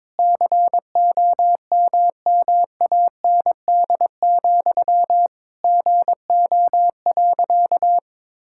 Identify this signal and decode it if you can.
Morse code, heard as a keyed tone, about 22 wpm, 700 Hz. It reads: COMMAND, GO.
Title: Sentences from Top 1600 Words 22wpm